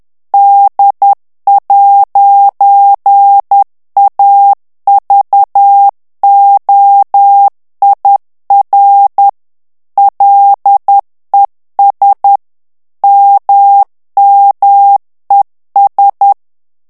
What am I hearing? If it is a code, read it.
Morse code, heard as a keyed tone, about 10 wpm, 800 Hz. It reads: D'AVOIR LES MMES
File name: morse mp3 / FRANCAIS TEXTE CLAIR 2 VITESSE 10 MOTS MINUTES